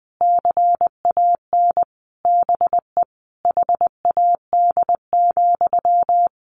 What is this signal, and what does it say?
Morse code, heard as a keyed tone, about 20 wpm, 700 Hz. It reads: CAN BE HAD,
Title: Sentences from Top 1500 Words 20wpm